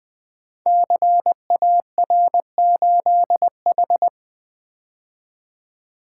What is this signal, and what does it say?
Morse code, heard as a keyed tone, about 21 wpm, 700 Hz. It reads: CAR8H
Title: Code Groups - 5 Letter:Numbers - 20wpm